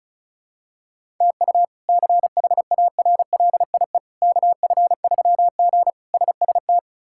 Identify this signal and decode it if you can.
Morse code, heard as a keyed tone, about 36 wpm, 700 Hz. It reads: TU CHARLIE KF3G SST
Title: K1USN SST QSO Practice - 35wpm